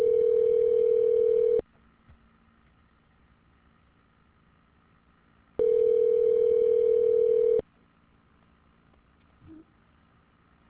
Ringback tone (North America), 440 + 480 Hz.